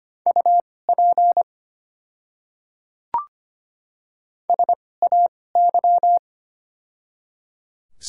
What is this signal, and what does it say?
Morse code, heard as a keyed tone, about 25 wpm, spaced as for 18 wpm, 700 Hz. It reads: UP SAY
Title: 2x Character Spacing - Top 300 Words - 25wpm